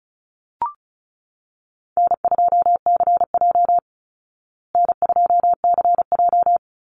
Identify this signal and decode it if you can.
Morse code, heard as a keyed tone, about 34 wpm, 700 Hz. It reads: N2CJ N2CJ
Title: Call Signs Repeated 3x - 35wpm